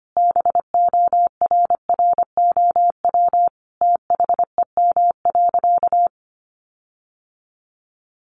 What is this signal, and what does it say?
Morse code, heard as a keyed tone, about 25 wpm, 700 Hz. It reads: BORROW THEM.
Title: Sentences from Top 1500 Words 25wpm